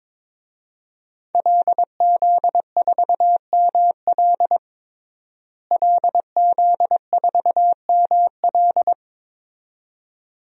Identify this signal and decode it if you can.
Morse code, heard as a keyed tone, about 22 wpm, 700 Hz. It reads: LZ4ML LZ4ML
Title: DXCC-100 Call Signs - 5 Characters - Repeated 2x - 22wpm